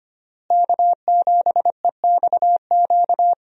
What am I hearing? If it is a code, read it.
Morse code, heard as a keyed tone, about 25 wpm, 700 Hz. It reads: K7EXQ
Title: USA Call Signs - 5 Characters - Repeated 3x - 25wpm